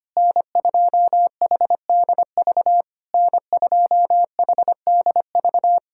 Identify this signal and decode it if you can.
Morse code, heard as a keyed tone, about 25 wpm, 700 Hz. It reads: N2HDV N2HDV
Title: SOTA QSO Practice 25wpm